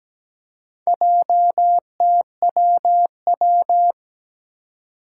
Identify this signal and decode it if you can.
Morse code, heard as a keyed tone, about 17 wpm, 700 Hz. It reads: JTWW
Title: Four Challenging Letter-Number Combinations 17wpm